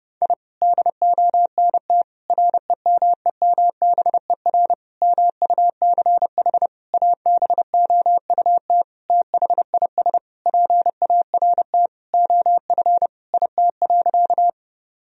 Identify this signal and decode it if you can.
Morse code, heard as a keyed tone, about 30 wpm, 700 Hz. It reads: I DONT REMEMBER MUCH ABOUT THIS PART OF IT.
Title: Sentences from Top 1000 Words - Review Part 1 - 30wpm